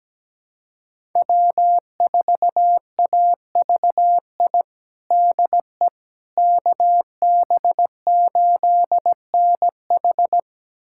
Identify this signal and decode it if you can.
Morse code, heard as a keyed tone, about 17 wpm, 700 Hz. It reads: W4AVI DE KB8NH
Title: USA Callsign DE Callsign - 5 Characters - 17wpm